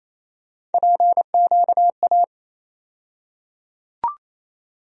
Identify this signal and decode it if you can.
Morse code, heard as a keyed tone, about 28 wpm, 700 Hz. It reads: PQA
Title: USA Call Sign Postfixes - 3 Letters 28wpm